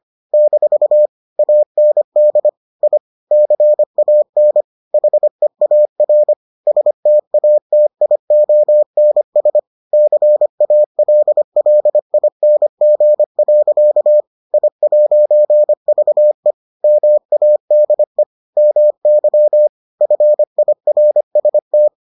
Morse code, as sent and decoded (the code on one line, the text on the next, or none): -....- / .- -. -.. / .. / -.-. .- -. / .... .- .-. / ... - .- - .. --- -. ... / -.-. .- .-.. .-.. .. -. --. .-.-.- / .. .----. ...- . / -- .- -.. / -- -.-- / ..-. .. .-. ... -
- AND I CAN HEAR STATIONS CALLING. I'VE MADE MY FIRST